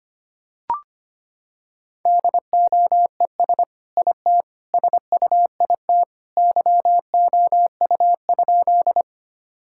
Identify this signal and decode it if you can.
Morse code, heard as a keyed tone, about 25 wpm, 700 Hz. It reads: DOES IT SUIT YOU?
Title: Sentences from Top 1600 Words 25wpm